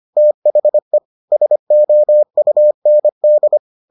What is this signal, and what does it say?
Morse code, heard as a keyed tone, about 25 wpm, 600 Hz. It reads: THE SOUND